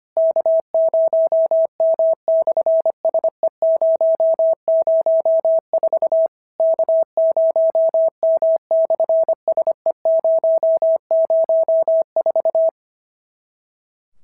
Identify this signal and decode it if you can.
Morse code, heard as a keyed tone, about 25 wpm, 650 Hz. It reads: K0M/SE004 K0M/SE004
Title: SOTA QSO Practice 25wpm